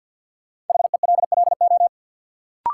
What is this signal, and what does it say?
Morse code, heard as a keyed tone, about 53 wpm, 700 Hz. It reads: HELLO
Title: Sentences from Top 1100 Words - New Word Drill Alt - 50wpm